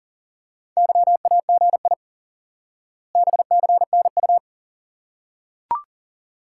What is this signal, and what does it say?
Morse code, heard as a keyed tone, about 40 wpm, 700 Hz. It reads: YAGI BCNU
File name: IQE Course 3 - Lesson 89 - YAGI - Two Elements - 40wpm